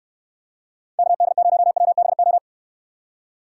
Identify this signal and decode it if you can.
Morse code, heard as a keyed tone, about 68 wpm, 700 Hz. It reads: KD0WBO